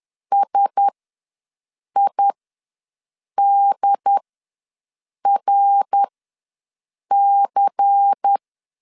Morse code, heard as a keyed tone, about 10 wpm, spaced as for 6 wpm, 800 Hz. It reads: SIDRC
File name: LEZIONI / 6 PKX0 / 6 PKX0 ws 10wpm 4220